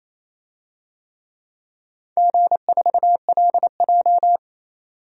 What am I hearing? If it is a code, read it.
Morse code, heard as a keyed tone, about 28 wpm, 700 Hz. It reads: G4LJ
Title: DXCC-100 Call Signs - 4 Characters - 28wpm